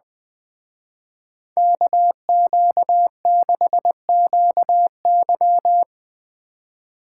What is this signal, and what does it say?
Morse code, heard as a keyed tone, about 22 wpm, 700 Hz. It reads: KQ6QY
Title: Speed Racing - USA 2x2 Call Signs 20wpm